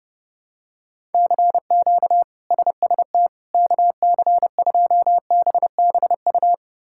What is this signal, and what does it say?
Morse code, heard as a keyed tone, about 32 wpm, 700 Hz. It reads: CQ SST KC2BBU
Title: K1USN SST QSO Practice - 30wpm